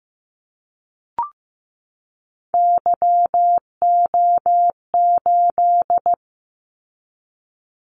Morse code, heard as a keyed tone, about 15 wpm, 700 Hz. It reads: YO8
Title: Callsign Prefixes for Top 25 DXCC - 15wpm